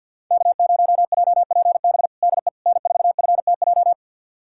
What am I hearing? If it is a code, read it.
Morse code, heard as a keyed tone, about 51 wpm, 700 Hz. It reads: K0JPB DE N4FTJ
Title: USA Callsign DE Callsign - 5 Characters - 50wpm